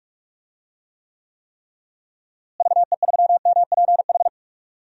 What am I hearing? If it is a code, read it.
Morse code, heard as a keyed tone, about 46 wpm, 700 Hz. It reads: VE3MPH
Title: DXCC-100 Call Signs - 6 Characters - Repeated 3x - 45wpm